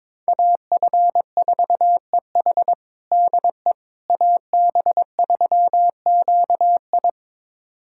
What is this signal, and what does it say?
Morse code, heard as a keyed tone, about 22 wpm, 700 Hz. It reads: AF4EH DE AB3QI
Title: USA Callsign DE Callsign - 5 Characters - 22wpm